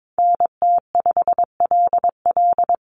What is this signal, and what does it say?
Morse code, heard as a keyed tone, about 22 wpm, 700 Hz. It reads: NT5LL
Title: USA Call Signs Repeated 3x 22wpm